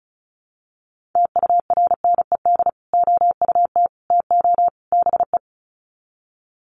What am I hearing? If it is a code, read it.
Morse code, heard as a keyed tone, about 34 wpm, 700 Hz. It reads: TURNED OUT TO BE
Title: Sets of 4 Words 35wpm